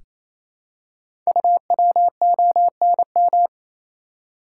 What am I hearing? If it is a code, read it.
Morse code, heard as a keyed tone, about 29 wpm, 700 Hz. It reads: UWONM
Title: Code Groups - 5 Letters - 28wpm